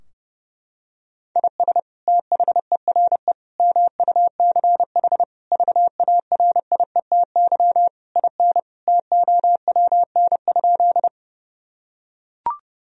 Morse code, heard as a keyed tone, about 30 wpm, 700 Hz. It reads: IS THERE MUCH VARIETY IN TOWN?
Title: Sentences from Top 1000 Words - Review Part 4 - 30wpm